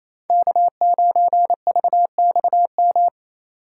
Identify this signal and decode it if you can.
Morse code, heard as a keyed tone, about 27 wpm, 700 Hz. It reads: K9VXM
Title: USA Call Signs - 5 Characters - 4x Character Spacing - 28wpm